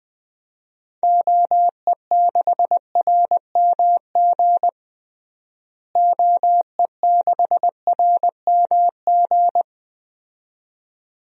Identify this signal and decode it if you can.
Morse code, heard as a keyed tone, about 20 wpm, 700 Hz. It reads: OE6RMG OE6RMG
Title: DXCC-100 Call Signs - 6 Characters - Repeated 2x - 20wpm